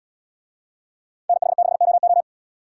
Morse code, heard as a keyed tone, about 78 wpm, 700 Hz. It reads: NH6OO